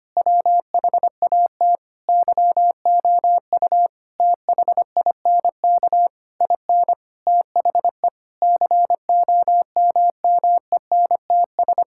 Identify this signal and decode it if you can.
Morse code, heard as a keyed tone, about 25 wpm, 700 Hz. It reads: WHAT YOU THINK IN THE COMMENTS
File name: practice / 25wpm / Sentences with 9-Word Phrases - 25wpm